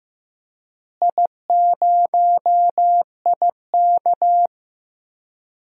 Morse code, heard as a keyed tone, about 15 wpm, 700 Hz. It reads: I0IK
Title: DXCC-100 Call Signs - 4 Characters - Repeated 2x - 15wpm